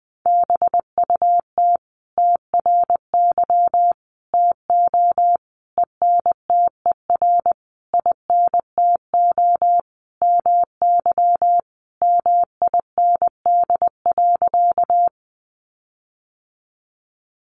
Morse code, heard as a keyed tone, about 20 wpm, 700 Hz. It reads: BUT TRY TO ENTER INTO MY MIND.
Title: Sentences from Top 1300 Words 20wpm